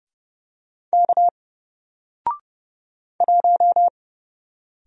Morse code, heard as a keyed tone, about 30 wpm, 700 Hz. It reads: K 1
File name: Single-Letter-Number-30wpm